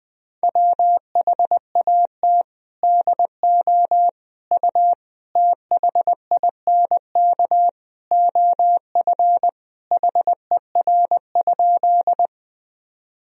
Morse code, heard as a keyed tone, about 20 wpm, 700 Hz. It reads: WHAT DO U THINK OF HER?
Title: Sentences from Top 200 Words and Abbreviations 20wpm